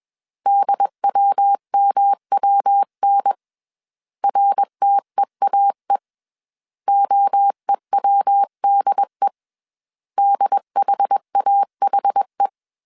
Morse code, heard as a keyed tone, about 21 wpm, 800 Hz. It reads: DWMWN RTEAE OEWDE DHAHE